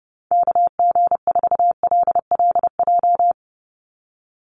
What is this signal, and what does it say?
Morse code, heard as a keyed tone, about 30 wpm, 700 Hz. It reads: KG4LLJ
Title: USA Call Signs 30wpm